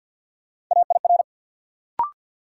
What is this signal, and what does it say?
Morse code, heard as a keyed tone, about 52 wpm, 700 Hz. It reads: AIR
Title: Sentences from Top 200 Words - New Word Drill - 50wpm